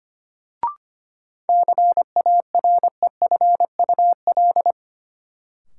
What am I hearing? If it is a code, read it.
Morse code, heard as a keyed tone, about 25 wpm, 700 Hz. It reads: CAREFUL
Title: Sentences from Top 1600 Words - New Word Drill - 25wpm